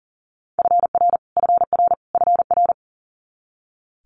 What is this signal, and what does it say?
Morse code, heard as a keyed tone, about 40 wpm, 690 Hz. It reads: FR FR FR
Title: Multi-letter Phonemes Repeated 3x 40wpm